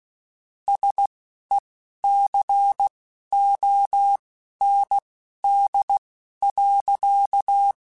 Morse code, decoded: SECOND.

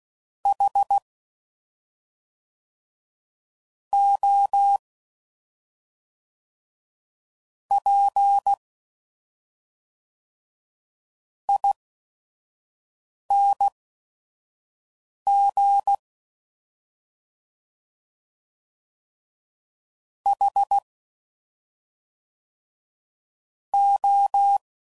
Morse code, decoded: HOPING HO

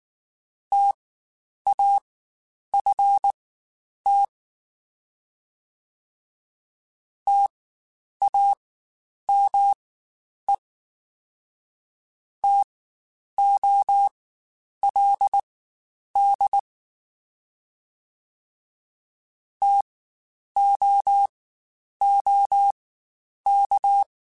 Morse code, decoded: TAFT TAME TOLD TOOK

